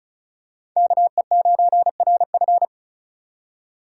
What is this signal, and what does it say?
Morse code, heard as a keyed tone, about 34 wpm, 700 Hz. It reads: KE9RF
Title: USA Call Signs - 5 Characters - 6x Character Spacing - 35wpm